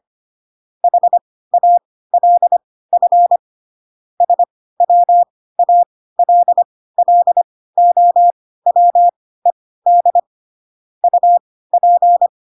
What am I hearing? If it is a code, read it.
Morse code, heard as a keyed tone, about 25 wpm, spaced as for 16 wpm, 700 Hz. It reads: HALF SWALLOWED UP
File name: Pinocchio / Pinocchio ch3 10 25 ik1hge